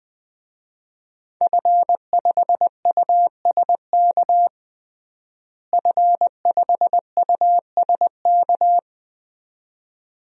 Morse code, heard as a keyed tone, about 20 wpm, 700 Hz. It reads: F5USK F5USK
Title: DXCC-100 Call Signs - 5 Characters - Repeated 2x - 20wpm